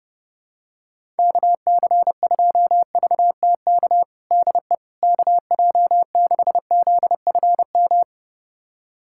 Morse code, heard as a keyed tone, about 30 wpm, 700 Hz. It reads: KC2VTK DE KJ6ZFM
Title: USA Callsign DE Callsign 30wpm